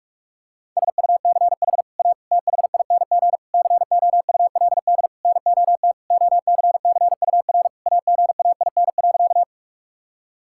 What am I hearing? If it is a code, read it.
Morse code, heard as a keyed tone, about 46 wpm, 700 Hz. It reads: SUCH A THING COULD NOT OCCUR AGAIN.
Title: Sentences from Top 1000 Words - Review Part 6 - 45wpm